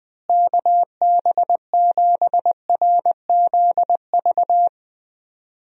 Morse code, heard as a keyed tone, about 20 wpm, 700 Hz. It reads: KB7RZV
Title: USA Call Signs - 6 Characters - Repeated 2x - 20wpm